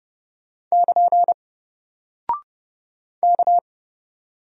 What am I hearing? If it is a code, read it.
Morse code, heard as a keyed tone, about 29 wpm, 700 Hz. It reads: <KN> K
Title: Single Character (Letter:Number:Punctuation:Prosign) Alt 30wpm